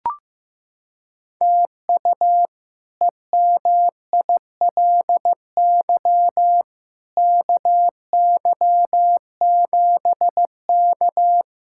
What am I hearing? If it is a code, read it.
Morse code, heard as a keyed tone, about 15 wpm, 700 Hz. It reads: TU EMILY KY7K
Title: K1USN SST QSO Practice - 15wpm